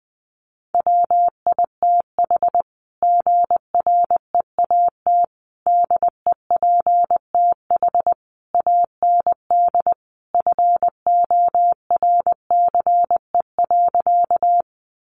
Morse code, heard as a keyed tone, about 20 wpm, 700 Hz. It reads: WITH GREAT DEPTH AND FORCE.
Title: Sentences from Top 1200 Words 20wpm